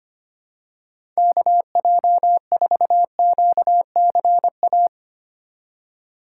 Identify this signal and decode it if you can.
Morse code, heard as a keyed tone, about 25 wpm, 700 Hz. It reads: KJ4QCA
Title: USA Call Signs - 6 Characters - 3x Character Spacing - 25wpm